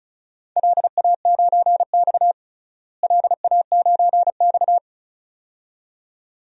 Morse code, heard as a keyed tone, about 34 wpm, 700 Hz. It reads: LA9X LA9X
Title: DXCC-100 Call Signs - 4 Characters - Repeated 2x - 35wpm